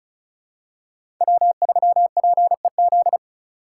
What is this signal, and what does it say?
Morse code, heard as a keyed tone, about 34 wpm, 700 Hz. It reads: W3PEZ
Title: USA Call Signs - 5 Characters - 2x Character Spacing - 35wpm